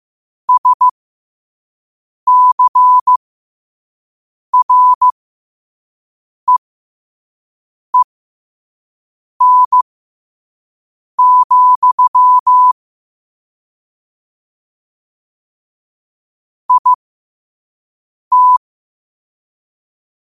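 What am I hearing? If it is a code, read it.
Morse code, heard as a keyed tone, about 15 wpm, spaced as for 5 wpm, 1000 Hz. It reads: SCREEN, IT
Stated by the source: Morse code at 5 wpm.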